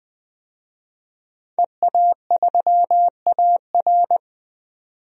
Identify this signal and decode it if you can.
Morse code, heard as a keyed tone, about 21 wpm, 700 Hz. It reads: EA3AR
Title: DXCC-100 Call Signs - 5 Characters - 20wpm